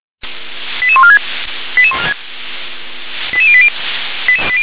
Five-tone selective call (ZVEI), heard as quick runs of five tones, tones of about 70 ms. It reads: 88146 88080